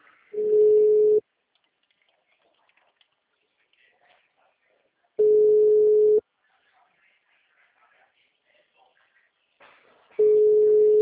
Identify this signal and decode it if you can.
Ringback tone (Europe, 425 Hz).
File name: out- -2017- - - .47376